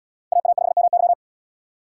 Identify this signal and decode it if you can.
Morse code, heard as a keyed tone, about 73 wpm, 700 Hz. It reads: AA5MY